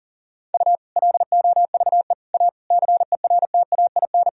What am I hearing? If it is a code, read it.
Morse code, heard as a keyed tone, about 40 wpm, 700 Hz. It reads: U LOVE A CERTAIN
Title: Sentences from Top 500 Words and Abbreviations 40wpm